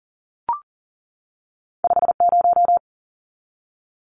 Morse code, heard as a keyed tone, about 40 wpm, 700 Hz. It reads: 50
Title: Numbers - 2-Digit 40wpm